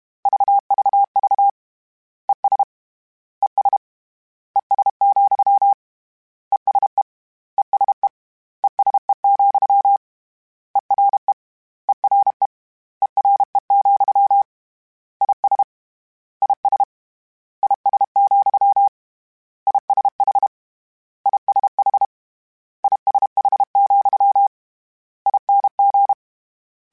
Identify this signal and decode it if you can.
Morse code, heard as a keyed tone, about 32 wpm, 800 Hz. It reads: VVV ES ES ES, ESE ESE ESE, ERE ERE ERE, IS IS IS, ISH ISH ISH, ING
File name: General / General-mp3s / General @30